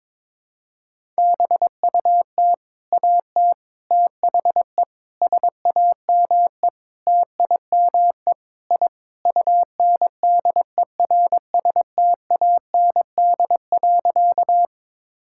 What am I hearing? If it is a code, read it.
Morse code, heard as a keyed tone, about 22 wpm, 700 Hz. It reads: BUT AT THE SAME TIME I UNDERSTAND.
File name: practice / 22wpm / Sentences with 4-Word Phrases - 22wpm